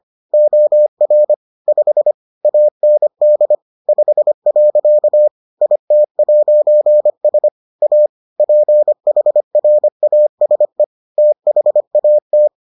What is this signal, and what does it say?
Morse code, heard as a keyed tone, about 25 wpm, 600 Hz. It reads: OR 5 AND 5. IT'S A PHRASE THAT